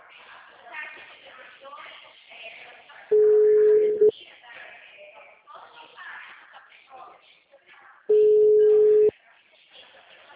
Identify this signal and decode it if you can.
Ringback tone (Europe, 425 Hz), with speech.